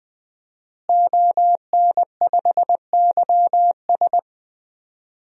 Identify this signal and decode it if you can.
Morse code, heard as a keyed tone, about 20 wpm, 700 Hz. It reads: ON5YS
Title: DXCC-100 Call Signs - 5 Characters - Repeated 3x - 20wpm